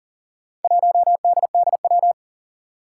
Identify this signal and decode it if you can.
Morse code, heard as a keyed tone, about 38 wpm, 700 Hz. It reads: 1DDW
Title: Four Challenging Letter-Number Combinations 40wpm